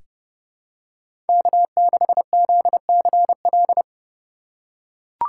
Morse code, heard as a keyed tone, about 31 wpm, 700 Hz. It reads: K6ZCL
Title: K1USN SST QSO Practice - 30wpm